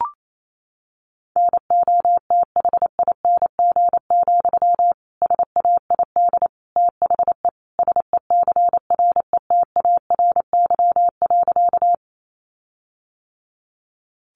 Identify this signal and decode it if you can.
Morse code, heard as a keyed tone, about 28 wpm, 700 Hz. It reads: NOTHING, SAID THE SECRETARY.
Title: Sentences from Top 1300 Words 28wpm